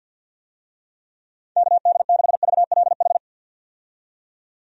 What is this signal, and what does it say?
Morse code, heard as a keyed tone, about 49 wpm, 700 Hz. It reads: KD6VLH
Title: USA Call Signs - 6 Characters - 50wpm